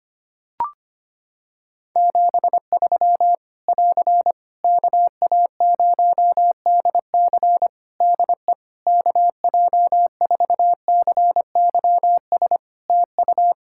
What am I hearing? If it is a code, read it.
Morse code, heard as a keyed tone, about 25 wpm, 700 Hz. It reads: 73 <AR> KA0DC DE KJ4CYS TU
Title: K1USN SST QSO Practice 25wpm